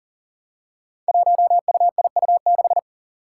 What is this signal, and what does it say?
Morse code, heard as a keyed tone, about 41 wpm, 690 Hz. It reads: 1UIU6
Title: Five Challenging Letter-Number Combinations 40wpm